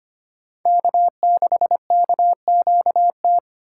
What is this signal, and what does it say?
Morse code, heard as a keyed tone, about 25 wpm, 700 Hz. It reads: K6KQT
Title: USA Call Signs - 5 Characters - 4x Character Spacing - 25wpm